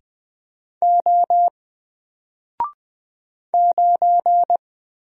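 Morse code, heard as a keyed tone, about 20 wpm, 700 Hz. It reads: O 9
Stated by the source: Single letter and number Morse drill at 20 wpm, from N4LCD.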